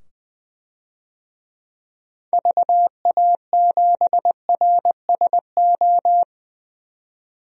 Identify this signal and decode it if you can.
Morse code, heard as a keyed tone, about 20 wpm, 700 Hz. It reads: VA7RSO
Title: DXCC-100 Call Signs - 6 Characters - 20wpm